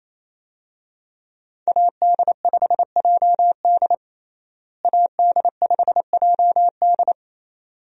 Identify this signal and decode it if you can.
Morse code, heard as a keyed tone, about 28 wpm, 700 Hz. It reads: AD5JD AD5JD
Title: USA Call Signs - 5 Characters - Repeated 2x - 28wpm